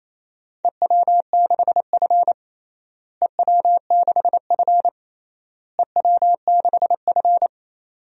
Morse code, heard as a keyed tone, about 28 wpm, 700 Hz. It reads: EW6F EW6F EW6F
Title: DXCC-100 Call Signs - 4 Characters - Repeated 3x - 28wpm